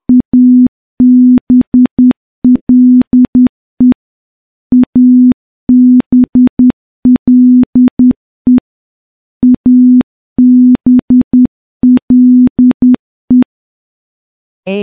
Morse code, heard as a keyed tone, about 10 wpm, 250 Hz. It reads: ABLE ABLE ABLE
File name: morse-training-10WPM